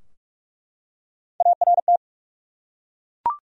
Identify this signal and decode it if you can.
Morse code, heard as a keyed tone, about 48 wpm, 700 Hz. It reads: ART
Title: USA Common Names - Max 3 Letters - 45wpm